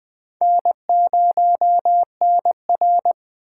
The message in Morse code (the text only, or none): N0NR